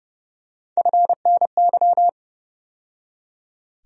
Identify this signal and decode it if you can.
Morse code, heard as a keyed tone, about 30 wpm, 700 Hz. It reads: FNY
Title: USA Call Sign Postfixes - 3 Letters 30wpm